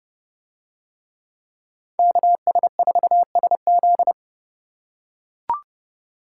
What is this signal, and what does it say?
Morse code, heard as a keyed tone, about 29 wpm, 700 Hz. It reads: KS4SZ